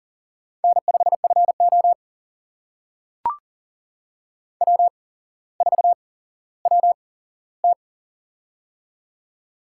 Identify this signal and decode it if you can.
Morse code, heard as a keyed tone, about 40 wpm, spaced as for 17 wpm, 700 Hz. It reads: N5FO W4WT
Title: USA Call Signs - 4 Characters - 8x Character Spacing - 40wpm